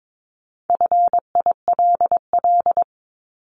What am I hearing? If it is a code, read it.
Morse code, heard as a keyed tone, about 22 wpm, 700 Hz. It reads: FILL
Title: Top 500 Words Repeated 22wpm